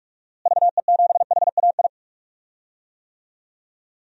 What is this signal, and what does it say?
Morse code, heard as a keyed tone, about 45 wpm, 700 Hz. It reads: VE7HAI
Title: DXCC-100 Call Signs - 6 Characters - Repeated 2x - 45wpm